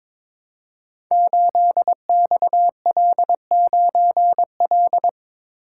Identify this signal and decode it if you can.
Morse code, heard as a keyed tone, about 22 wpm, 700 Hz. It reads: 8XL9L
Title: Code Groups - 5 Letter:Numbers - 22wpm